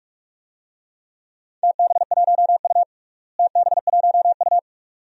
Morse code, heard as a keyed tone, about 46 wpm, 700 Hz. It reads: TB1U TB1U
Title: DXCC-100 Call Signs - 4 Characters - Repeated 2x - 45wpm